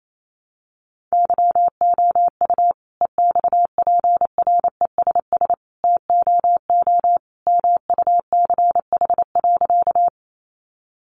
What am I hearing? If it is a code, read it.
Morse code, heard as a keyed tone, about 28 wpm, 700 Hz. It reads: YOU EXPRESS TOO MUCH.
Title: Sentences from Top 1100 Words 28wpm